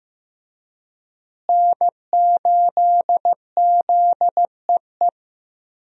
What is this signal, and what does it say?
Morse code, heard as a keyed tone, about 15 wpm, 700 Hz. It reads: N8ZEE
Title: USA Call Signs - 5 Characters - Repeated 3x - 15wpm